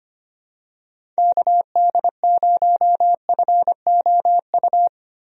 Morse code, heard as a keyed tone, about 25 wpm, 700 Hz. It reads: KD0FOU
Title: USA Call Signs - 6 Characters - Repeated 2x - 25wpm